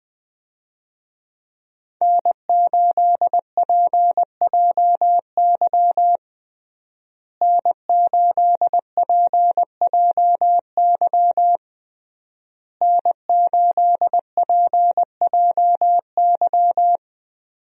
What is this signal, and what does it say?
Morse code, heard as a keyed tone, about 19 wpm, 700 Hz. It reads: N8PJY N8PJY N8PJY
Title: USA Call Signs - 5 Characters - Repeated 3x - 20wpm